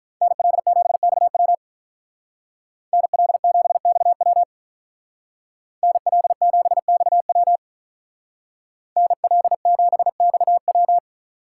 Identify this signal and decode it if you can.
Morse code, heard as a keyed tone, about 45 wpm, 700 Hz. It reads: NL7XW NL7XW NL7XW NL7XW